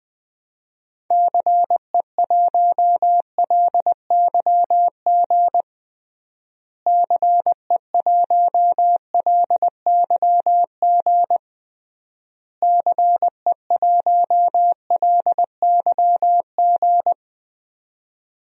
Morse code, heard as a keyed tone, about 20 wpm, 700 Hz. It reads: CE1LYG CE1LYG CE1LYG
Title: DXCC-100 Call Signs - 6 Characters - Repeated 3x - 20wpm